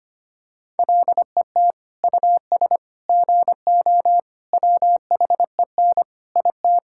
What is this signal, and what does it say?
Morse code, heard as a keyed tone, about 25 wpm, 700 Hz. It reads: LET US GO WHEN IT
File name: practice / 25wpm / Sentences with 2-Word Phrases - 25wpm